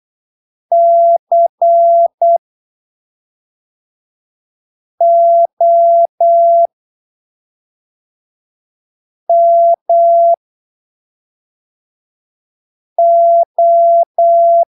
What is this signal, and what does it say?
Morse code, heard as a keyed tone, about 8 wpm, spaced as for 3 wpm, 680 Hz. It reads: COMO
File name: arquivos / texto / texto 01 08ppm